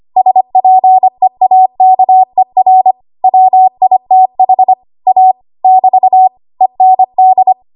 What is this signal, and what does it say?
Morse code, heard as a keyed tone, about 25 wpm, 750 Hz. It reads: SPEAKER WITH A <BT> END
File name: Archive / 25 WPM / 25WPM